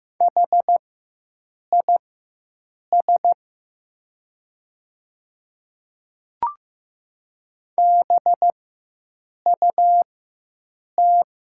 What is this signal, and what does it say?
Morse code, heard as a keyed tone, about 15 wpm, spaced as for 7 wpm, 700 Hz. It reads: HIS BUT
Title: 4x Character Spacing - Top 300 Words - 15wpm